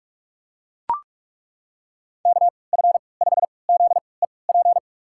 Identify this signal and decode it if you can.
Morse code, heard as a keyed tone, about 41 wpm, spaced as for 25 wpm, 700 Hz. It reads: KF5ZEP
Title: USA Call Signs - 6 Characters - 3x Character Spacing - 45wpm